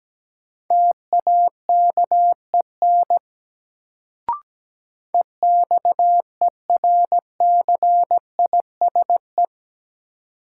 Morse code, heard as a keyed tone, about 17 wpm, 700 Hz. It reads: TAKEN EXERCISE
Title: Sentences from Top 1000 Words - Review Part 4 - New Word Drill - 17wpm